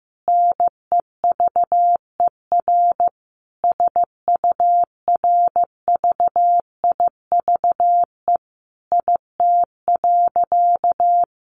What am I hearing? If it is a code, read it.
Morse code, heard as a keyed tone, about 15 wpm, 700 Hz. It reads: NEVER SURVIVE IT.
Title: Sentences from Top 1300 Words 15wpm